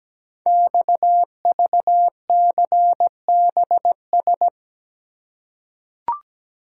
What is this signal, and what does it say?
Morse code, heard as a keyed tone, about 17 wpm, 700 Hz. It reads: XVCBS
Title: Code Groups - 5 Letters - 17wpm